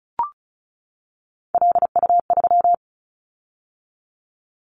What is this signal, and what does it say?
Morse code, heard as a keyed tone, about 36 wpm, 700 Hz. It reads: LU3
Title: Callsign Prefixes for Top 100 DXCC - 35wpm